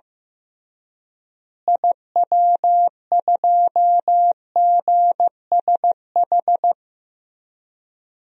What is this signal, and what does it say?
Morse code, heard as a keyed tone, about 15 wpm, 700 Hz. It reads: IW2GSH
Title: DXCC-100 Call Signs - 6 Characters - Repeated 3x - 15wpm